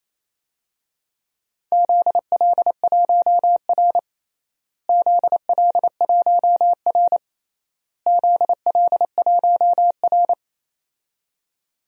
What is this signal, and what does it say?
Morse code, heard as a keyed tone, about 28 wpm, 700 Hz. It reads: ZL1R ZL1R ZL1R
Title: DXCC-100 Call Signs - 4 Characters - Repeated 3x - 28wpm